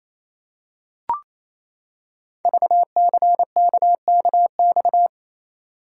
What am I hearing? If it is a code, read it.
Morse code, heard as a keyed tone, about 28 wpm, 690 Hz. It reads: VCKKX
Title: Code Groups - 5 Letters - 28wpm